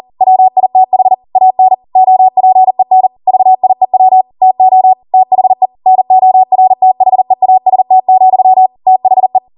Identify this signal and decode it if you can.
Morse code, heard as a keyed tone, about 40 wpm, 750 Hz. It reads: WITH AN OPEN VIEW TO THE NORTHEAST, THE